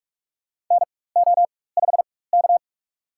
Morse code, heard as a keyed tone, about 44 wpm, spaced as for 21 wpm, 700 Hz. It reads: NO5K
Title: USA Call Signs - 4 Characters - 4x Character Spacing - 45wpm